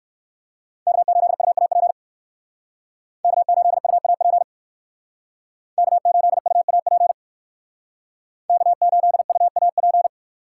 Morse code, heard as a keyed tone, about 58 wpm, 700 Hz. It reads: K8UAP K8UAP K8UAP K8UAP